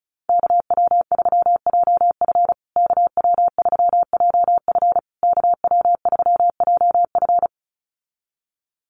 Morse code, heard as a keyed tone, about 34 wpm, 700 Hz. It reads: KW3JF KW3JF KW3JF
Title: USA Call Signs Repeated 3x 35wpm